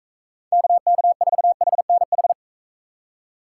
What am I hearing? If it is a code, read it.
Morse code, heard as a keyed tone, about 43 wpm, 700 Hz. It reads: KK4HNH